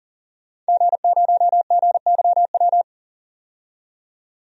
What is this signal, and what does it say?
Morse code, heard as a keyed tone, about 40 wpm, 700 Hz. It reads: G0GYW
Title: DXCC-100 Call Signs - 5 Characters - 40wpm